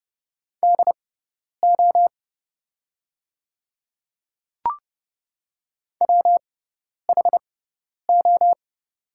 Morse code, heard as a keyed tone, about 31 wpm, spaced as for 10 wpm, 700 Hz. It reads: DO WHO
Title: 6x Character Spacing - Top 100 Words - 30wpm